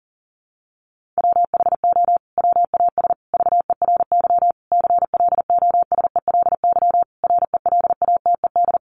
Morse code, heard as a keyed tone, about 37 wpm, 700 Hz. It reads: WHO WAS VERY CLOSELY RELATED
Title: Sentences from Top 1400 Words 40wpm